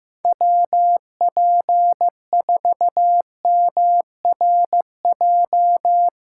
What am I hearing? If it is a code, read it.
Morse code, heard as a keyed tone, about 16 wpm, 700 Hz. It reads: WP4MRJ
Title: Speed Racing - USA 2x3 Call Signs 15wpm